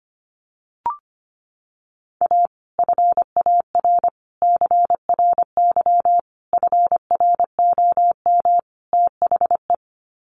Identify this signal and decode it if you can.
Morse code, heard as a keyed tone, about 25 wpm, 700 Hz. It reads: A FAR CRY FROM THE
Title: Sets of 5 Words Encore 25wpm